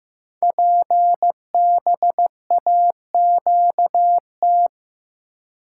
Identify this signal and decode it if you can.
Morse code, heard as a keyed tone, about 15 wpm, 700 Hz. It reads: PBAQT
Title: Code Groups - 5 Letters - 15wpm